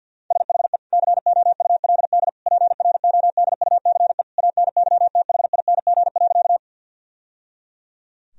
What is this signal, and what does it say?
Morse code, heard as a keyed tone, about 47 wpm, 700 Hz. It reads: SHE COULD PRODUCE ANYTHING.
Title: Sentences from Top 1000 Words - Review Part 3 - 50wpm